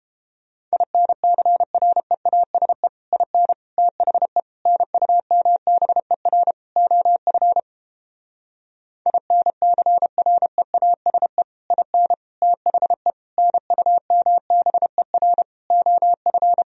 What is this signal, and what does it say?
Morse code, heard as a keyed tone, about 31 wpm, 700 Hz. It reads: INCREASE IN THE NUMBER OF INCREASE IN THE NUMBER OF